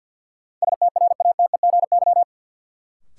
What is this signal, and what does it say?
Morse code, heard as a keyed tone, about 46 wpm, 700 Hz. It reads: STRATEGY
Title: Sentences from Top 1000 Words - Review Part 5 - New Word Drill - 50wpm